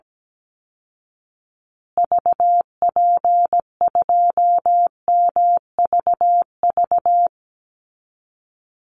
Morse code, heard as a keyed tone, about 17 wpm, 700 Hz. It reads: VP2MVV
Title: Call Signs Repeated 2x - 17wpm